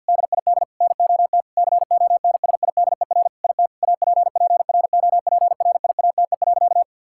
Morse code, heard as a keyed tone, about 50 wpm, 700 Hz. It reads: DID NOT CONSIDER IT APPROPRIATE.
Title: Sentences from Top 1000 Words - Review Part 5 - 50wpm